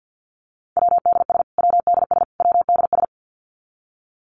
Morse code, heard as a keyed tone, about 50 wpm, 700 Hz. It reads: WDS WDS WDS
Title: Abbreviations Repeated 3x - 50wpm